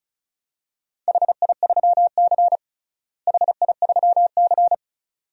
Morse code, heard as a keyed tone, about 36 wpm, 700 Hz. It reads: HI3C HI3C
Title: DXCC-100 Call Signs - 4 Characters - Repeated 2x - 35wpm